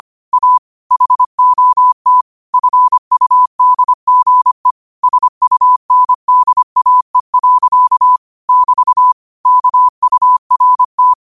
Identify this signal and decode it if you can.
Morse code, heard as a keyed tone, about 25 wpm, 1000 Hz. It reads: A HOT FUDGE SUNDAE. <BT> KURT
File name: 25wpm